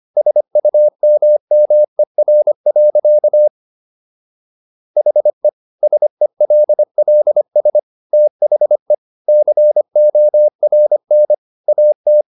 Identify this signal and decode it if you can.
Morse code, heard as a keyed tone, about 25 wpm, 600 Hz. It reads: SUMMER. HE SELLS THE CORN AT
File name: SS111 25